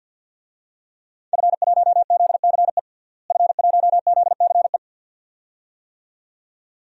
Morse code, heard as a keyed tone, about 53 wpm, 700 Hz. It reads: F1ZCE F1ZCE
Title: DXCC-100 Call Signs - 5 Characters - Repeated 2x - 50wpm